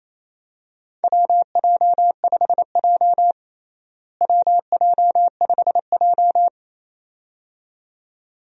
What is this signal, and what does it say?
Morse code, heard as a keyed tone, about 28 wpm, 700 Hz. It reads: WJ5J WJ5J
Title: USA Call Signs - 4 Characters - Repeated 2x - 28wpm